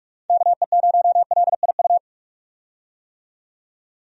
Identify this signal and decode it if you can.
Morse code, heard as a keyed tone, about 44 wpm, 700 Hz. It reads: KE0LIU
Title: USA Call Signs - 6 Characters - 45wpm